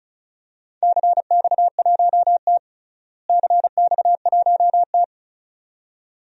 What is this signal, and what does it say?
Morse code, heard as a keyed tone, about 34 wpm, 700 Hz. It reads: CX1T CX1T
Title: DXCC-100 Call Signs - 4 Characters - Repeated 2x - 35wpm